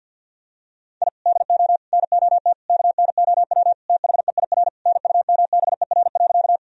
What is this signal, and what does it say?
Morse code, heard as a keyed tone, about 50 wpm, 700 Hz. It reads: I DO NOT KNOW THEIR NUMBER.
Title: Sentences from Top 1000 Words - Review Part 1 - 50wpm